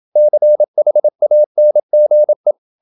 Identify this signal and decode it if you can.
Morse code, heard as a keyed tone, about 27 wpm, 600 Hz. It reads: CHANGE
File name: Never 0027wpm